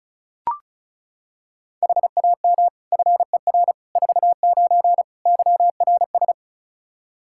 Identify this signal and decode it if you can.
Morse code, heard as a keyed tone, about 36 wpm, 700 Hz. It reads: HAM FER 49 YRS
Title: K1USN SST QSO Practice 35wpm